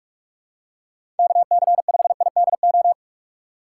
Morse code, heard as a keyed tone, about 46 wpm, 700 Hz. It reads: KC5IDO
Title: USA Call Signs - 6x Character Spacing - 45wpm